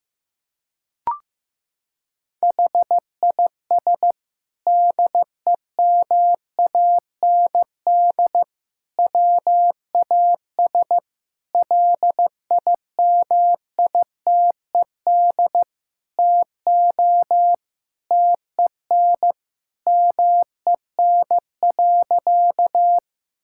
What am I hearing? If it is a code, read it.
Morse code, heard as a keyed tone, about 15 wpm, 700 Hz. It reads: HIS DEMAND WAS LIMITED TO TEN MEN.
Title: Sentences from Top 1000 Words - Review Part 6 - 15wpm